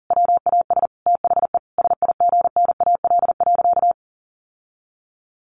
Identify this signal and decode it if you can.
Morse code, heard as a keyed tone, about 40 wpm, 700 Hz. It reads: WAS THE SIGNAL.
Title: Sentences from Top 1400 Words 40wpm